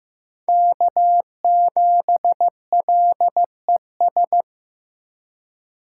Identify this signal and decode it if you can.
Morse code, heard as a keyed tone, about 15 wpm, 700 Hz. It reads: K7LES
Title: USA Call Signs - 5 Characters - 2x Character Spacing - 15wpm